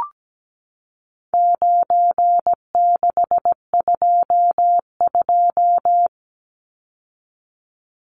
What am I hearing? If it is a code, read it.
Morse code, heard as a keyed tone, about 17 wpm, 700 Hz. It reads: 9622
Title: Numbers - 4-Digit 17wpm